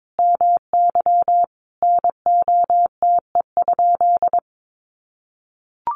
Morse code, heard as a keyed tone, about 22 wpm, 700 Hz. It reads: MY NOTE?
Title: Sentences from Top 500 Words 22wpm